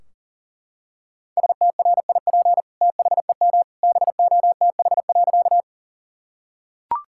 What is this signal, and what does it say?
Morse code, heard as a keyed tone, about 40 wpm, 700 Hz. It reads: STRIP THEM BOTH.
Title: Sentences from Top 1600 Words 40wpm